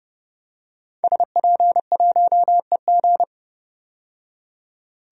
Morse code, heard as a keyed tone, about 30 wpm, 700 Hz. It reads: SP1EG
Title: DXCC-100 Call Signs - 5 Characters - 30wpm